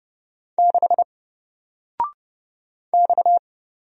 Morse code, heard as a keyed tone, about 30 wpm, 700 Hz. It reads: 6 X
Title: Single Character (Letter:Number:Punctuation:Prosign) 30wpm